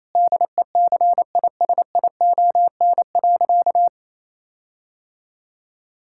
Morse code, heard as a keyed tone, about 27 wpm, 700 Hz. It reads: DECISION.
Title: Sentences from Top 1600 Words 28wpm